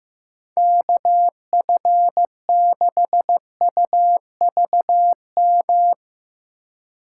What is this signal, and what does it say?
Morse code, heard as a keyed tone, about 15 wpm, 700 Hz. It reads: KF6UVM
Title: USA Call Signs - 6 Characters - 4x Character Spacing - 15wpm